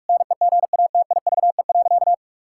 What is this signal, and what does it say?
Morse code, heard as a keyed tone, about 46 wpm, 700 Hz. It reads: NEGATIVE.
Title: Sentences from Top 1000 Words - Review Part 6 - 45wpm